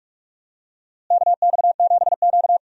Morse code, heard as a keyed tone, about 46 wpm, 700 Hz. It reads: KX7Q
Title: USA Call Signs - 4 Characters - Repeated 3x - 45wpm